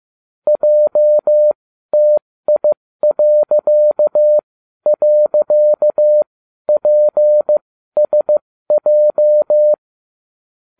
Morse code, heard as a keyed tone, about 15 wpm, spaced as for 12 wpm, 600 Hz. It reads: JTI..PSJ